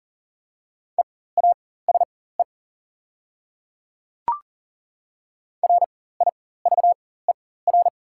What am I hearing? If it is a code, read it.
Morse code, heard as a keyed tone, about 40 wpm, spaced as for 19 wpm, 700 Hz. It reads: EASE RIVER
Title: 4x Character Spacing - Top 300 Words - 40wpm